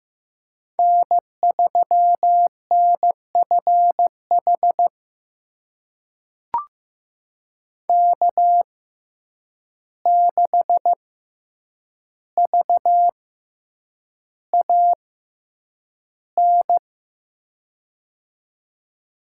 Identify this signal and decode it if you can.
Morse code, heard as a keyed tone, about 15 wpm, spaced as for 8 wpm, 700 Hz. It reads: N3NFH K6VAN
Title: USA Call Signs - 5 Characters - 6x Character Spacing - 15wpm